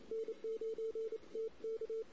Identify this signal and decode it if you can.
Morse code, heard as a keyed tone, about 28 wpm, 450 Hz. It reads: N9TK
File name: CW / 2008-ss-cw-k5tr-op-n5rz -16:00